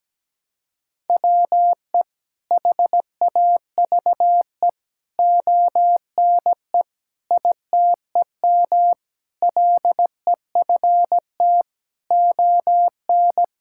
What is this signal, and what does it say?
Morse code, heard as a keyed tone, about 17 wpm, 700 Hz. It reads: WE HAVE ONE ITEM LEFT ON
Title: Sentences from Top 1000 Words - Review Part 6 - 17wpm